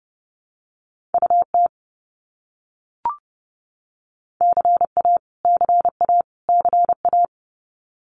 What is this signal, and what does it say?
Morse code, heard as a keyed tone, about 30 wpm, 700 Hz. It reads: UT CA CA CA
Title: US States Repeated 3x 30wpm